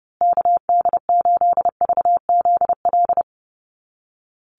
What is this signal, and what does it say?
Morse code, heard as a keyed tone, about 30 wpm, 700 Hz. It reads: KD8VZL
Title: USA Call Signs Repeated 3x 30wpm